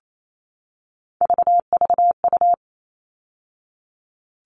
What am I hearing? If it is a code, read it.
Morse code, heard as a keyed tone, about 28 wpm, 700 Hz. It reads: VVU
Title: All Cut Numbers - 3-Digit 28wpm